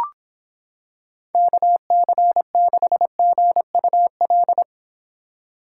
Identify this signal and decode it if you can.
Morse code, heard as a keyed tone, about 26 wpm, 700 Hz. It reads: KC6GUL